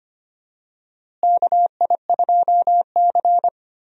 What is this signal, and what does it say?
Morse code, heard as a keyed tone, about 25 wpm, 700 Hz. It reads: KI2C
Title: USA Call Signs - 4 Characters - Repeated 2x - 25wpm